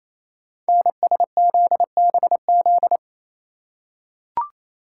Morse code, heard as a keyed tone, about 27 wpm, 700 Hz. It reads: NSZBZ
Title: Code Groups - 5 Letters - 28wpm